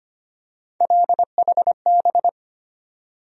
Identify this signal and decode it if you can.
Morse code, heard as a keyed tone, about 25 wpm, 700 Hz. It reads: LHB
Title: USA Call Sign Postfixes - 3 Letters 25wpm